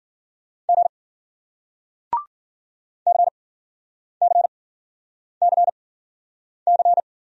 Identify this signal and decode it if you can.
Morse code, heard as a keyed tone, about 49 wpm, spaced as for 9 wpm, 700 Hz. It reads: G CCCC
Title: Speed-Racing Single Letters 40wpm